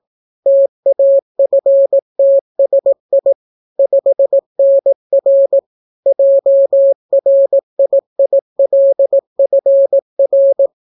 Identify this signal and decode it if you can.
Morse code, heard as a keyed tone, about 18 wpm, 550 Hz. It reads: TAFTSI 5NR JRIILFR